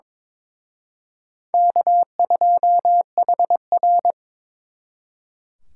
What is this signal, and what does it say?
Morse code, heard as a keyed tone, about 26 wpm, 700 Hz. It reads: K2HR
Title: Speed Racing - USA 1x2 Call Signs 22wpm